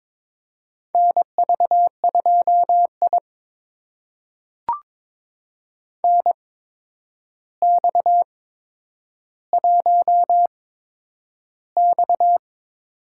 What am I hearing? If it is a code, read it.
Morse code, heard as a keyed tone, about 22 wpm, spaced as for 9 wpm, 700 Hz. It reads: NV2I NX1X
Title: USA Call Signs - 4 Characters - 8x Character Spacing - 22wpm